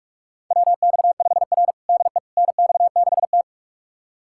Morse code, heard as a keyed tone, about 46 wpm, 700 Hz. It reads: WX5R DE NX6T
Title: USA Callsign DE Callsign - 4 Characters - 45wpm